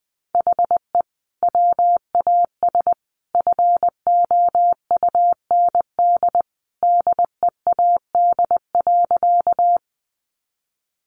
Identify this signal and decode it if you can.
Morse code, heard as a keyed tone, about 20 wpm, 700 Hz. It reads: HE WAS FOUND DEAD.
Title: Sentences from Top 1500 Words 20wpm